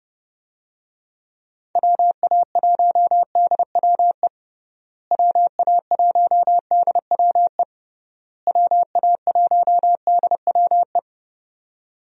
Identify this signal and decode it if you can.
Morse code, heard as a keyed tone, about 30 wpm, 700 Hz. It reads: WA1DWE WA1DWE WA1DWE
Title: USA Call Signs - 6 Characters - Repeated 3x - 30wpm